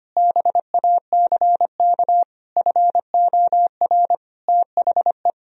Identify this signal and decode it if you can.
Morse code, heard as a keyed tone, about 25 wpm, 700 Hz. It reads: BACK FOR THE
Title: Sentences from Top 1000 Words - Review Part 4 - 25wpm